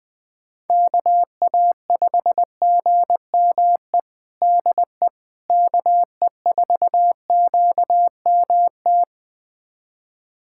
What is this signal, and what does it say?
Morse code, heard as a keyed tone, about 21 wpm, 700 Hz. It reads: KA5GME DE KE4QMT
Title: USA Callsign DE Callsign - 6 Characters - 20wpm